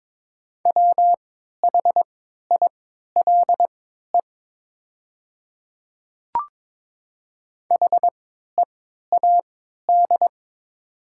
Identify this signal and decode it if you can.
Morse code, heard as a keyed tone, about 22 wpm, spaced as for 12 wpm, 700 Hz. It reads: WHILE HEAD
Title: 3x Character Spacing - Top 300 Words - 22wpm